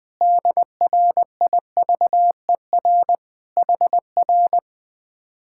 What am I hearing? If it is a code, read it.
Morse code, heard as a keyed tone, about 20 wpm, 700 Hz. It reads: DRIVER HR
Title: Speed-Racing Common QSO Phrases - 17wpm